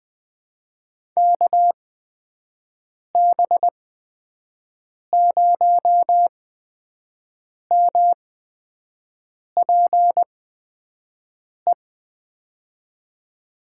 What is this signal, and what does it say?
Morse code, heard as a keyed tone, about 20 wpm, spaced as for 5 wpm, 700 Hz. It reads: KB0MPE
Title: USA Call Signs - 6 Characters - 8x Character Spacing - 20wpm